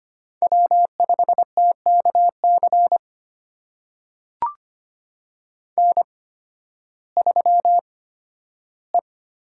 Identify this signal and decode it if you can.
Morse code, heard as a keyed tone, about 25 wpm, spaced as for 13 wpm, 700 Hz. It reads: W5TKC N3E
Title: USA Call Signs - 5 Characters - 8x Character Spacing - 25wpm